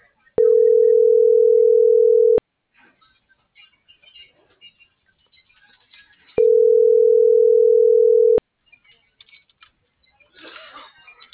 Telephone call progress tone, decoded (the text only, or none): ringback tone